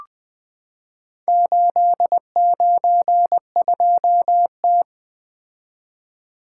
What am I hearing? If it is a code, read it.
Morse code, heard as a keyed tone, about 20 wpm, 700 Hz. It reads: 892T
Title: Four Letter-Number Combinations - 20wpm